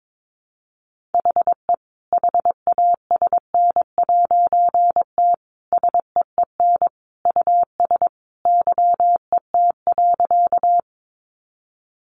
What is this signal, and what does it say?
Morse code, heard as a keyed tone, about 22 wpm, 700 Hz. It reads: HE HASN'T SEEN US YET.
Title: A Sound Of Thunder 22wpm